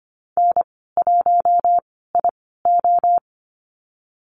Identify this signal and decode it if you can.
Morse code, heard as a keyed tone, about 25 wpm, spaced as for 16 wpm, 700 Hz. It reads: N1IO
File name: Lesson 15g - Number 5 - Call Signs 25wpm